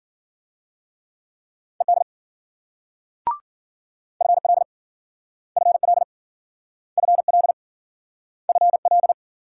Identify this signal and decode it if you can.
Morse code, heard as a keyed tone, about 53 wpm, 700 Hz. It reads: ED FL FL FL FL